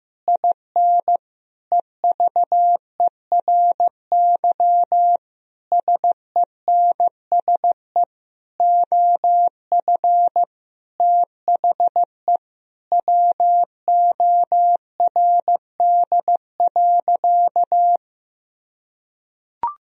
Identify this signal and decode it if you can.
Morse code, heard as a keyed tone, about 14 wpm, 700 Hz. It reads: IN EVERY SENSE OF THE WORD.
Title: Sentences from Top 1000 Words - Review Part 4 - 15wpm